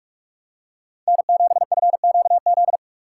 Morse code, heard as a keyed tone, about 42 wpm, 700 Hz. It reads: N7FQZ
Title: USA Call Signs - 4x Character Spacing - 45wpm